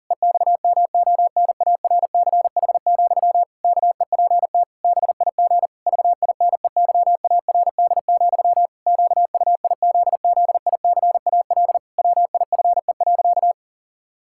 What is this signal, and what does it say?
Morse code, heard as a keyed tone, about 37 wpm, 700 Hz. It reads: EXMONARCH, KEPT BIG VINEYARD, QUIZZICAL WIFE.
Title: Pangram Challenge v2 - 40wpm